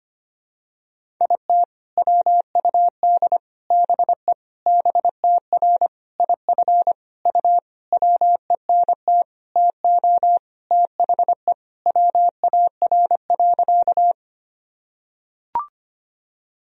Morse code, heard as a keyed tone, about 25 wpm, 700 Hz. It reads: IT WUD BE BTR IF U WENT TO THE WAR.
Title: Sentences from Top 400 Words and Abbreviations 25wpm